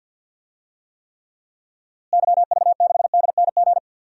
Morse code, heard as a keyed tone, about 47 wpm, 700 Hz. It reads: YV6DNG
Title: DXCC-100 Call Signs - 6 Characters - Repeated 2x - 50wpm